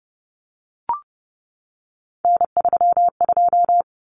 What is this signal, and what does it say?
Morse code, heard as a keyed tone, about 30 wpm, 700 Hz. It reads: N32
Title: Common Cut Numbers - 3-Digit 30wpm